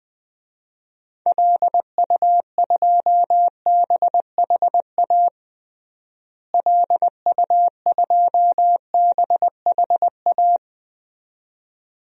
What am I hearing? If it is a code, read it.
Morse code, heard as a keyed tone, about 21 wpm, 700 Hz. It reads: LU2BHA LU2BHA
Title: DXCC-100 Call Signs - 6 Characters - Repeated 2x - 20wpm